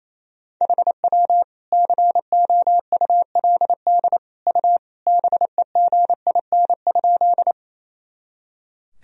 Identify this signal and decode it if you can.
Morse code, heard as a keyed tone, about 28 wpm, 700 Hz. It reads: HW COULD U BEGIN?
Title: Sentences from Top 300 Words and Abbreviations 28wpm